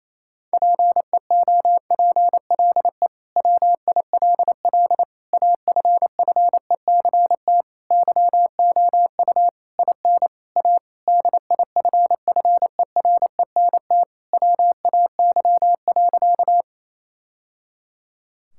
Morse code, heard as a keyed tone, about 28 wpm, 700 Hz. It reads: PEOPLE WILL AFFECT YOU IN A DIFFERENT WAY.
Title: Sentences from Top 1000 Words - Review Part 6 - 28wpm